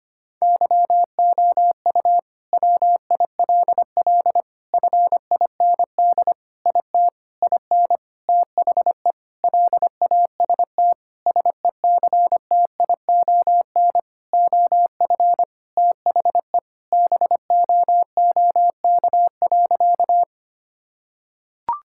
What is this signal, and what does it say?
Morse code, heard as a keyed tone, about 25 wpm, 700 Hz. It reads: YOU WILL FIND IT IN THE LAST SECTION OF THE BOOK.
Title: Sentences from Top 1000 Words - Review Part 5 - 25wpm